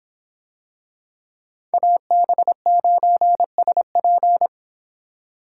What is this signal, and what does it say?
Morse code, heard as a keyed tone, about 26 wpm, 700 Hz. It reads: AB9SP